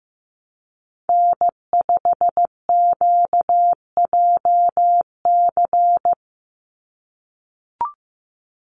Morse code, heard as a keyed tone, about 15 wpm, 700 Hz. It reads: N5QJC
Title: USA Call Signs 15wpm